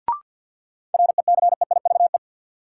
Morse code, heard as a keyed tone, about 46 wpm, 700 Hz. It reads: RECEIVE
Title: Sentences from Top 1000 Words - New Word Drill - 50wpm